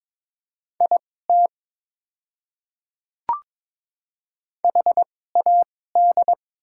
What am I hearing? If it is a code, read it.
Morse code, heard as a keyed tone, about 22 wpm, spaced as for 16 wpm, 700 Hz. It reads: IT HAD
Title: 2x Character Spacing - Top 200 Words - 22wpm